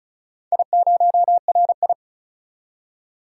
Morse code, heard as a keyed tone, about 36 wpm, 700 Hz. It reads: I0RI
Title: DXCC-100 Call Signs - 4 Characters - 35wpm